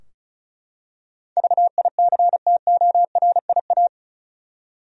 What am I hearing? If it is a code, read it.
Morse code, heard as a keyed tone, about 34 wpm, 700 Hz. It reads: VICTORIA
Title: USA Common Names - 35wpm